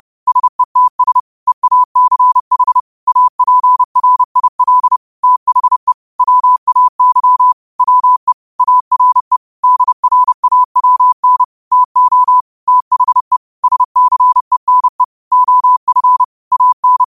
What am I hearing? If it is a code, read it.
Morse code, heard as a keyed tone, about 30 wpm, 1000 Hz. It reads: SETS EACH APRIL THE WAY WE ARE DRAWN TO THE SCENE OF AN
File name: qotdfiles / 30wpm